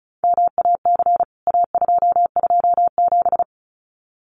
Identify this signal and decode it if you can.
Morse code, heard as a keyed tone, about 34 wpm, 700 Hz. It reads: MAC A227
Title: CWT Exchange 35wpm